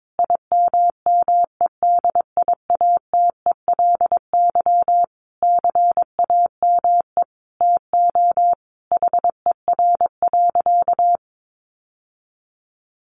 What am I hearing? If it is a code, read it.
Morse code, heard as a keyed tone, about 22 wpm, 700 Hz. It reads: IMMEDIATELY CAME TO HER.
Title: Sentences from Top 1100 Words 22wpm